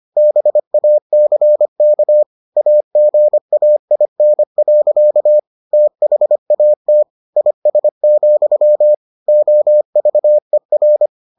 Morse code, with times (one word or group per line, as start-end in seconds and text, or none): 0.2-2.2 BACK
2.6-5.4 AGAIN.
5.7-7.0 THAT
7.4-8.9 IS,
9.3-11.1 OVER